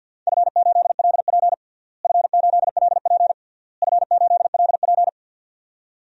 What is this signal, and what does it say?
Morse code, heard as a keyed tone, about 50 wpm, 700 Hz. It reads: F8LP F8LP F8LP
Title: DXCC-100 Call Signs - 4 Characters - Repeated 3x - 50wpm